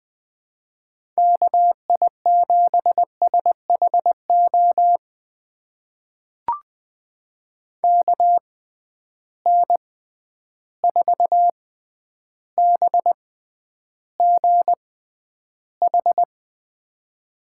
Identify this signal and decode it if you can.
Morse code, heard as a keyed tone, about 20 wpm, spaced as for 10 wpm, 700 Hz. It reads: KI7SHO KN4BGH
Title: USA Call Signs - 6 Characters - 6x Character Spacing - 20wpm